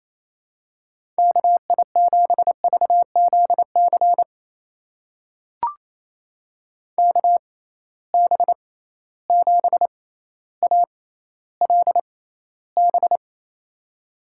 Morse code, heard as a keyed tone, about 28 wpm, spaced as for 14 wpm, 700 Hz. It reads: KI7VZC KB7ALB
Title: USA Call Signs - 6 Characters - 6x Character Spacing - 28wpm